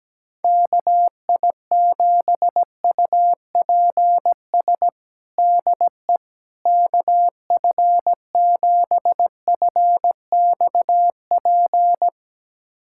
Morse code, heard as a keyed tone, about 17 wpm, 700 Hz. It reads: KI7UPS DE KF7FXP
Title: USA Callsign DE Callsign - 6 Characters - 17wpm